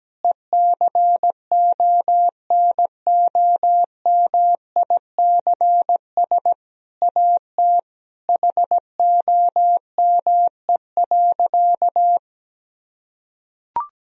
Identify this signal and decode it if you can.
Morse code, heard as a keyed tone, about 17 wpm, 700 Hz. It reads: ECONOMICS AT HOME.
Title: Sentences from Top 1000 Words - Review Part 3 - 17wpm